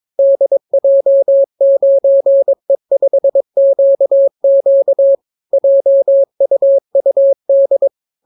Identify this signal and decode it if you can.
Morse code, heard as a keyed tone, about 22 wpm, 550 Hz. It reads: DJ9E5QQ JUUD